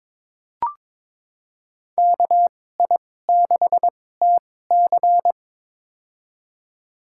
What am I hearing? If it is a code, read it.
Morse code, heard as a keyed tone, about 23 wpm, spaced as for 16 wpm, 700 Hz. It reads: KI6TC
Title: USA Call Signs - 2x Character Spacing - 22wpm